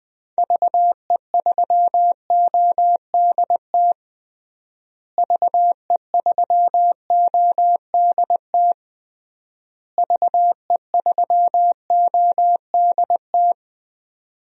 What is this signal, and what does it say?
Morse code, heard as a keyed tone, about 20 wpm, 700 Hz. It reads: VE3ODT VE3ODT VE3ODT
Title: DXCC-100 Call Signs - 6 Characters - Repeated 3x - 20wpm